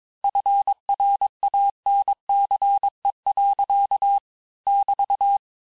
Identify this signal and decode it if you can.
Morse code, heard as a keyed tone, about 22 wpm, 800 Hz. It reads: FRANCE. <BT>
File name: cours-du-lundi files / cours-1587-texte-clair-vitesse-16mpm-puis-21mpm